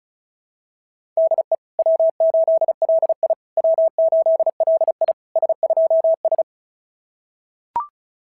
Morse code, heard as a keyed tone, about 36 wpm, 690 Hz. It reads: DE W8LI W8LI S2S
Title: SOTA QSO Practice 35wpm